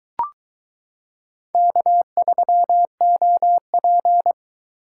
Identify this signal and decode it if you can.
Morse code, heard as a keyed tone, about 22 wpm, 700 Hz. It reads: K3OP